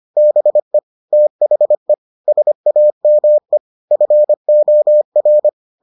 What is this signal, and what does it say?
Morse code, heard as a keyed tone, about 25 wpm, 600 Hz. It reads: BE THE SAME FOR